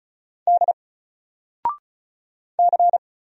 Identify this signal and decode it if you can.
Morse code, heard as a keyed tone, about 36 wpm, 700 Hz. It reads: D C
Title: Single Letter-Number 35wpm